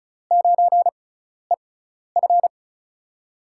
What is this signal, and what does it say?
Morse code, heard as a keyed tone, about 36 wpm, spaced as for 12 wpm, 700 Hz. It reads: 9EF
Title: USA Call Signs - 5 Characters - 6x Character Spacing - 35wpm